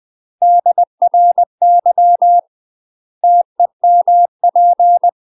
Morse code, heard as a keyed tone, about 20 wpm, 700 Hz. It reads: DRY TEMP
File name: UK MASTER QSOs TRAINING / 20WPM / UK MASTER QSOs 0025